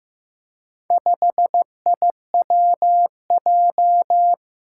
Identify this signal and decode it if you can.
Morse code, heard as a keyed tone, about 16 wpm, 700 Hz. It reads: 5IWJ
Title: Four Challenging Letter-Number Combinations 15wpm